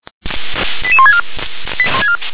Five-tone selective call (ZVEI), heard as quick runs of five tones, tones of about 70 ms. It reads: 88164
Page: SDR/ZVEI